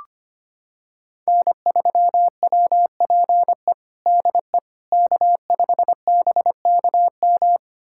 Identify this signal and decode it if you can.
Morse code, heard as a keyed tone, about 25 wpm, 700 Hz. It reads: N3WPE DE K5BKM
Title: USA Callsign DE Callsign - 5 Characters - 25wpm